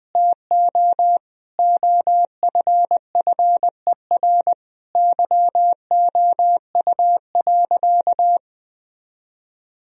Morse code, heard as a keyed tone, about 20 wpm, 700 Hz. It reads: TO OFFER YOU.
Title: Sentences from Top 1000 Words - Review Part 4 - 20wpm